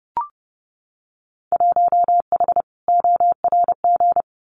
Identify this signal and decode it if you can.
Morse code, heard as a keyed tone, about 30 wpm, 700 Hz. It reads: 1H ORG
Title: Winter Field Day Exchanges 30wpm